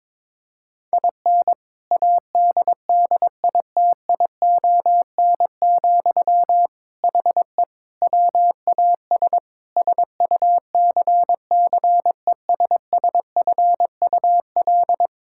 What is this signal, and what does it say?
Morse code, heard as a keyed tone, about 22 wpm, 700 Hz. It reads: IN ADDITION, HE WAS SUCCESSFUL
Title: Sentences from Top 1000 Words - Review Part 6 - 22wpm